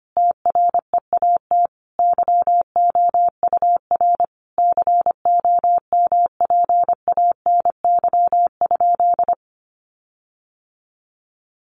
Morse code, heard as a keyed tone, about 25 wpm, 700 Hz. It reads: TREAT YOUR COMPANY?
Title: Sentences from Top 1300 Words 25wpm